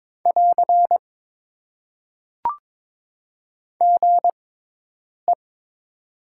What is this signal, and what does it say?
Morse code, heard as a keyed tone, about 22 wpm, spaced as for 8 wpm, 700 Hz. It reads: <AR> GE
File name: Top 90 QSO Elements - 6x Character Spacing - 22wpm